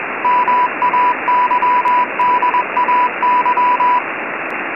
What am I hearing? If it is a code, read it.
Morse code, heard as a keyed tone, about 21 wpm, 990 Hz, with radio noise. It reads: MAYDAY
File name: Sos Mayday 988 Hz Freq 20wpm W Heavy Background Static